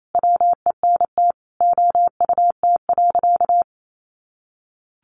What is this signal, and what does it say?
Morse code, heard as a keyed tone, about 28 wpm, 700 Hz. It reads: WENT OUT.
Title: Sentences from Top 300 Words 28wpm